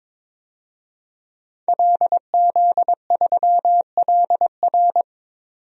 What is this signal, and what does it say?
Morse code, heard as a keyed tone, about 22 wpm, 700 Hz. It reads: LZ3LR
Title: DXCC-100 Call Signs - 5 Characters - 22wpm